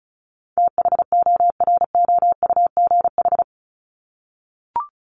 Morse code, heard as a keyed tone, about 34 wpm, 700 Hz. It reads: THOROUGH
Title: LY Suffix - Top 100 Words with Base Words 35wpm